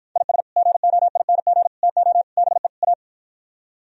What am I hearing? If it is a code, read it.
Morse code, heard as a keyed tone, about 52 wpm, 700 Hz. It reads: IS GOING TO BE A